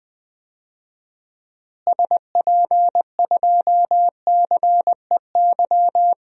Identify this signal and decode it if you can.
Morse code, heard as a keyed tone, about 20 wpm, 700 Hz. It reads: SP2CEY
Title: DXCC-100 Call Signs - 6 Characters - Repeated 3x - 20wpm